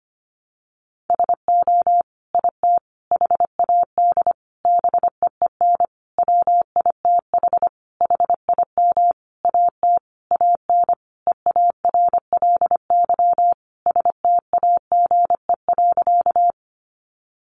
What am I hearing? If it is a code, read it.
Morse code, heard as a keyed tone, about 25 wpm, 700 Hz. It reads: SO IT HAD BEEN WITH HIM AT AN EARLY STAGE.
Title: Sentences from Top 1200 Words 25wpm